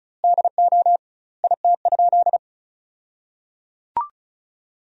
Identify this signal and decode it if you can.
Morse code, heard as a keyed tone, about 34 wpm, 700 Hz. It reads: DO IT?
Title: 2x Word-Spacing - Sentences from Top 200 Words - 35wpm